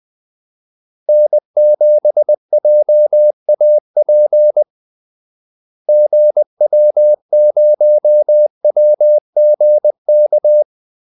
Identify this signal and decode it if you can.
Morse code, heard as a keyed tone, about 20 wpm, 600 Hz. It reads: N7JAP GW0WGK
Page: Kaldesignaler 20 wpm